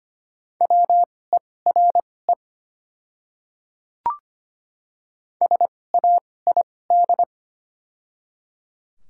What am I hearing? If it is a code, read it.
Morse code, heard as a keyed tone, about 25 wpm, spaced as for 18 wpm, 700 Hz. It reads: WERE SAID
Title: 2x Character Spacing - Top 100 Words - 25wpm